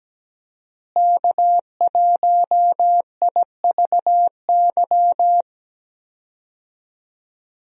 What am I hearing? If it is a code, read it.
Morse code, heard as a keyed tone, about 17 wpm, 700 Hz. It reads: K1IVY
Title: USA Call Signs - 5 Characters - Repeated 2x - 17wpm